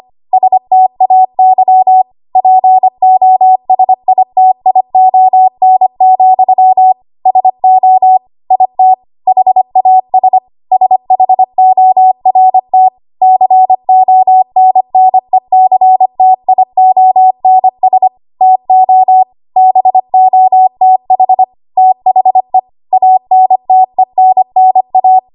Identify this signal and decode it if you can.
Morse code, heard as a keyed tone, about 25 wpm, 750 Hz. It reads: STAY POSITION, SO IT HAS SHORT CONNECTIONS TO BOTH THE ANTENNA
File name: Archive / 25 WPM / 25WPM